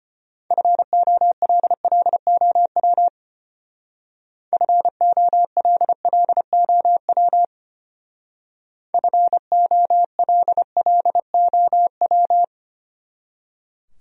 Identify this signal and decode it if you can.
Morse code, heard as a keyed tone, about 31 wpm, 700 Hz. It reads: FOLLOW FOLLOW FOLLOW
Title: Speed-Racing Top 200 Words 25wpm